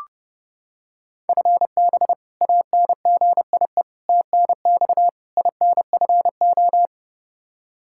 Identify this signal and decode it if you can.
Morse code, heard as a keyed tone, about 29 wpm, 700 Hz. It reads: FB ANGIE TNX INFO
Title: K1USN SST QSO Practice 30wpm